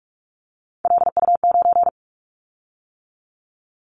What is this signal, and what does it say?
Morse code, heard as a keyed tone, about 46 wpm, 700 Hz. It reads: LU9
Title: Callsign Prefixes for Top 100 DXCC - 45wpm